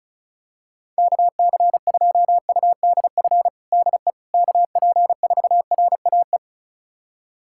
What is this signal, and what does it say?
Morse code, heard as a keyed tone, about 34 wpm, 700 Hz. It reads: KC2UDF DE KP4RAE
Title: USA Callsign DE Callsign - 6 Characters - 35wpm